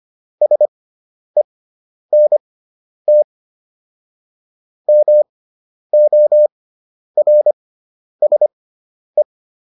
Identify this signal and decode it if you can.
Morse code, heard as a keyed tone, about 25 wpm, spaced as for 10 wpm, 600 Hz. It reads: SENT MORSE